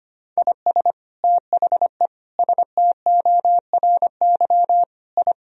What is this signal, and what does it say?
Morse code, heard as a keyed tone, about 25 wpm, 700 Hz. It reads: IS THE STORY I
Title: Sentences from Top 1000 Words - Review Part 2 - 25wpm